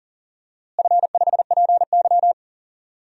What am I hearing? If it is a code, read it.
Morse code, heard as a keyed tone, about 40 wpm, 700 Hz. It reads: F5PY
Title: DXCC-100 Call Signs - 4 Characters - Repeated 2x - 40wpm